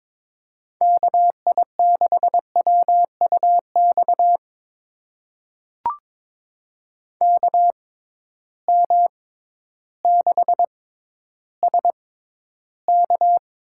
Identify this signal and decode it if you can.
Morse code, heard as a keyed tone, about 21 wpm, spaced as for 12 wpm, 700 Hz. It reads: KI6WUX KM6SK
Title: USA Call Signs - 6x Character Spacing - 22wpm